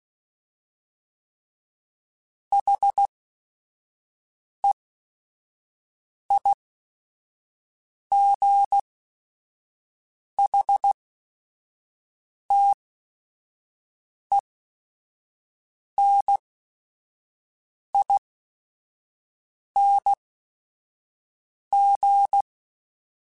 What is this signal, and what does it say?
Morse code, heard as a keyed tone, about 16 wpm, spaced as for 5 wpm, 800 Hz. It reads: HEIGHTENING
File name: Special INGs / Special INGs-mp3s / Sngl Ing @5